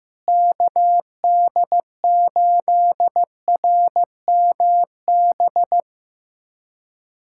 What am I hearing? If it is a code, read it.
Morse code, heard as a keyed tone, about 15 wpm, 700 Hz. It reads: KD8RMB
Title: USA Callsign DE Callsign 15wpm